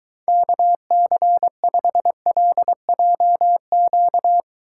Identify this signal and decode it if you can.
Morse code, heard as a keyed tone, about 23 wpm, 700 Hz. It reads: KC5LJQ